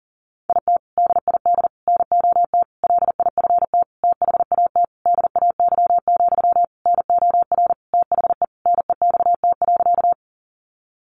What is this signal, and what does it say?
Morse code, heard as a keyed tone, about 40 wpm, 700 Hz. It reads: IT DID NOT LIFT THAT DAY, NOR THE NEXT.
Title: Sentences from Top 1500 Words 40wpm